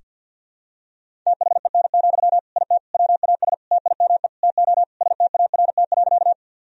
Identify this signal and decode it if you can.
Morse code, heard as a keyed tone, about 51 wpm, 700 Hz. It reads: THEN, IT WAS TIME TO START.
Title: Sentences from Top 1000 Words - Review Part 1 - 50wpm